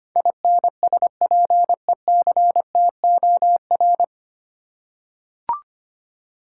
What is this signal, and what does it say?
Morse code, heard as a keyed tone, about 25 wpm, 700 Hz. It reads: INSPECTOR
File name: practice / 25wpm / 9 Letter Words - First 3 Letters Given - 25wpm